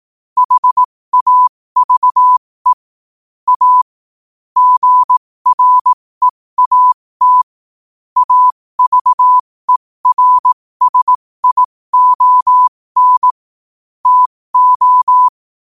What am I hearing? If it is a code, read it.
Morse code, heard as a keyed tone, about 18 wpm, spaced as for 15 wpm, 1000 Hz. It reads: HAVE A GREAT AVERSION TO
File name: qotdfiles / 15wpm